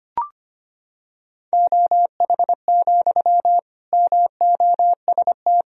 Morse code, heard as a keyed tone, about 25 wpm, 700 Hz. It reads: OH, MOST
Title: Sentences from Top 1000 Words - Review Part 1 - 25wpm